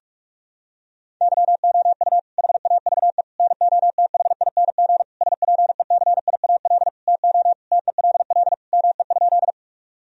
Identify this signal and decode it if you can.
Morse code, heard as a keyed tone, about 46 wpm, 700 Hz. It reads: YOU HAVE NOTHING SPECIAL TO TELL ME?
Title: Sentences from Top 1000 Words - Review Part 3 - 45wpm